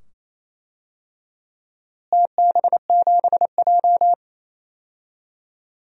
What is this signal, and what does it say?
Morse code, heard as a keyed tone, about 28 wpm, 700 Hz. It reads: TB7J
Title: DXCC-100 Call Signs - 4 Characters - Repeated 2x - 28wpm